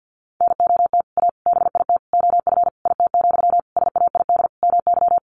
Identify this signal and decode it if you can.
Morse code, heard as a keyed tone, about 50 wpm, 700 Hz. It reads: NOT A BIT OF IT, SAID MY
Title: Sentences from Top 1400 Words 50wpm